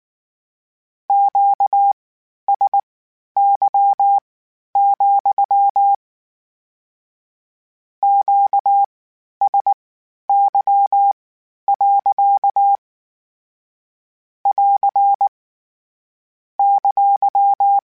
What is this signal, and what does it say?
Morse code, heard as a keyed tone, about 19 wpm, spaced as for 11 wpm, 800 Hz. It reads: QSY, QSY. <AR> !